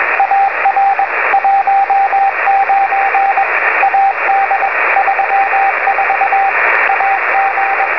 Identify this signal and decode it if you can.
Morse code, heard as a keyed tone, about 21 wpm, 780 Hz, with radio noise. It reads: AR 19 AD 34 AG